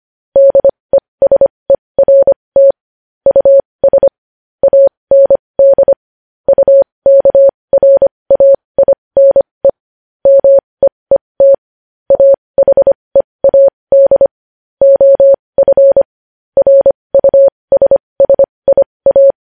Morse code, heard as a keyed tone, about 25 wpm, spaced as for 20 wpm, 550 Hz. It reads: DESERT US AND UKRAINE MEET AHEAD OF RUSSIA
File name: podcast / News headlines-1x-25x20